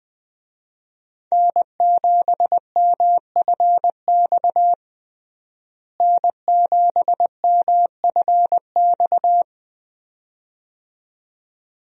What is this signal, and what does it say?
Morse code, heard as a keyed tone, about 20 wpm, 700 Hz. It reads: N7MFX N7MFX
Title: USA Call Signs - 5 Characters - Repeated 2x - 20wpm